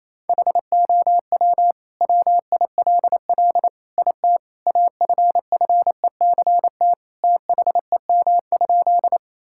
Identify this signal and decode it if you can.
Morse code, heard as a keyed tone, about 28 wpm, 700 Hz. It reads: HOW WILL IT AFFECT THEM?
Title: Sentences from Top 1000 Words - Review Part 6 - 28wpm